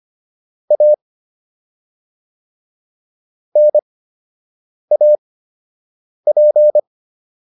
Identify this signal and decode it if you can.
Morse code, heard as a keyed tone, about 25 wpm, spaced as for 7 wpm, 600 Hz. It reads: A NAP